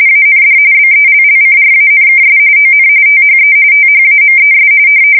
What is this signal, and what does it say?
RTTY, about 45.45 baud. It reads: AND BELLCRANKS ON THE THAMES
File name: fsk170 / usnavy5